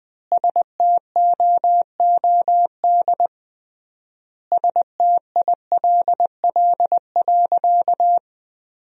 Morse code, heard as a keyed tone, about 20 wpm, 700 Hz. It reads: STOOD STILL.
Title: 3x Word-Spacing - Sentences from Top 500 Words - 20wpm